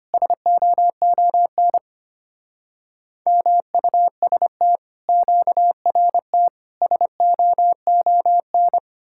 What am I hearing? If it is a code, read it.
Morse code, heard as a keyed tone, about 32 wpm, 700 Hz. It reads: SOON MUST QRT SOON